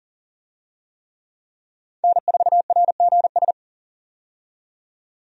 Morse code, heard as a keyed tone, about 40 wpm, 700 Hz. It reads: N4RGS
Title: USA Call Signs - 5 Characters - 40wpm